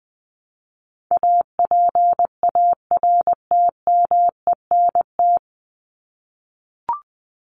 Morse code, heard as a keyed tone, about 20 wpm, 700 Hz. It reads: APARTMENT
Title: Top 751-1000 Words 20wpm